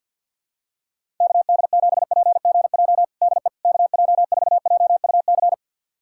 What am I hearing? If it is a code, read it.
Morse code, heard as a keyed tone, about 52 wpm, 700 Hz. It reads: KD7PGJ DE KJ4JUC
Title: USA Callsign DE Callsign 50wpm